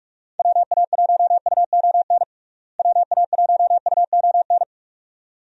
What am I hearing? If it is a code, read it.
Morse code, heard as a keyed tone, about 46 wpm, 700 Hz. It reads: WA1UON WA1UON
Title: USA Call Signs - 6 Characters - Repeated 2x - 45wpm